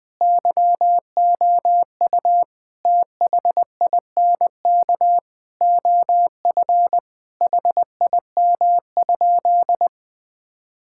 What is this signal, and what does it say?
Morse code, heard as a keyed tone, about 20 wpm, 700 Hz. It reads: YOU THINK OF HIM?
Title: Sentences from Top 1000 Words - Review Part 5 - 20wpm